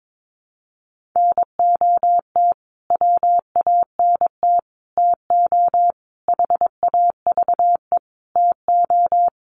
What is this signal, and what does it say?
Morse code, heard as a keyed tone, about 22 wpm, 700 Hz. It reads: NOT WANT TO HAVE TO
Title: Sets of 5 Words Encore2 22wpm